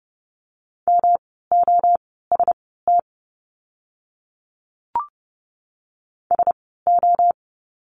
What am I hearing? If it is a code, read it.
Morse code, heard as a keyed tone, about 30 wpm, spaced as for 17 wpm, 700 Hz. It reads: MOST SO
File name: course / 30:10wpm / 108 Lesson 18f - Letter m - Words 30wpm